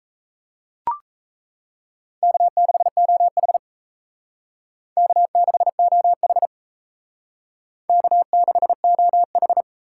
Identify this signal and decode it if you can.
Morse code, heard as a keyed tone, about 36 wpm, 700 Hz. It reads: K6OH K6OH K6OH